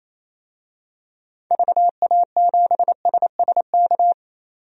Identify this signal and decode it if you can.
Morse code, heard as a keyed tone, about 28 wpm, 700 Hz. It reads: VA7SSK
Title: DXCC-100 Call Signs - 6 Characters - 28wpm